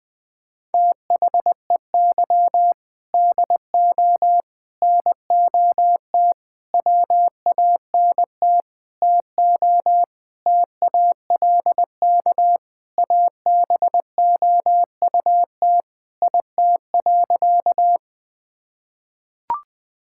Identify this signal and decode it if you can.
Morse code, heard as a keyed tone, about 20 wpm, 700 Hz. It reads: THEY DO NOT WANT TO TALK ABOUT IT.
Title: Sentences with 5-Word Phrases - Spoken Beginning - 20wpm